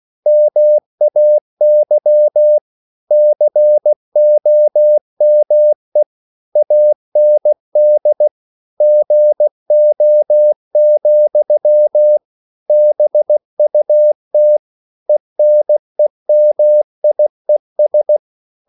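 Morse code, heard as a keyed tone, about 16 wpm, 600 Hz. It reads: MAY COME AND GO, BUT ENEMIES
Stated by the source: English quotations sent in CW as practice at 16 words per minute.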